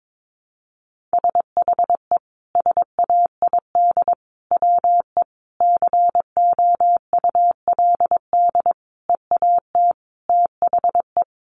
Morse code, heard as a keyed tone, about 22 wpm, 700 Hz. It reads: SHE SAID WE COULD EAT THE
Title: Sentences from Top 1300 Words 22wpm